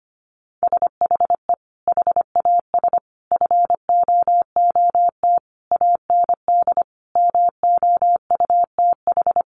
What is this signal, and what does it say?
Morse code, heard as a keyed tone, about 25 wpm, 700 Hz. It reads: SHE HAS FOOT AND MOUTH
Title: Sentences from Top 1500 Words 25wpm